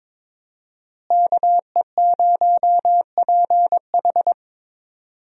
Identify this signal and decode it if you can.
Morse code, heard as a keyed tone, about 22 wpm, 700 Hz. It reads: KE0PH
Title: USA Call Signs - 5 Characters - Repeated 3x - 22wpm